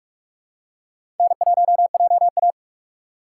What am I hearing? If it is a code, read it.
Morse code, heard as a keyed tone, about 41 wpm, 700 Hz. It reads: N1JA
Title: USA Call Signs - 4 Characters - 2x Character Spacing - 45wpm